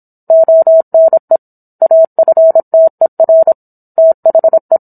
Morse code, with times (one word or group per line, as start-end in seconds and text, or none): 0.3-1.4 ONE
1.8-3.5 AFTER
4.0-4.8 THE